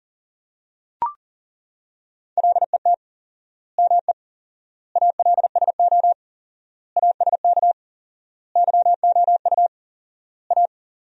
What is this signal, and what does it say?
Morse code, heard as a keyed tone, about 40 wpm, 700 Hz. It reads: LET ME ALSO ASK YOU A
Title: 4x Word-Spacing - Sentences from Top 400 Words - 40wpm